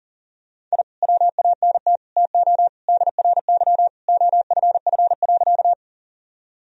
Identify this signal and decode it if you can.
Morse code, heard as a keyed tone, about 40 wpm, 700 Hz. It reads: I WANT TO DRY OFF.
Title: Sentences from Top 1000 Words - Review Part 3 - 40wpm